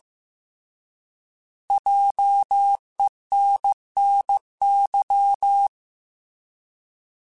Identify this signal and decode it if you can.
Morse code, heard as a keyed tone, about 15 wpm, 800 Hz. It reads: JENNY
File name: Alphabetical / Alphabetical mp3s / Js @13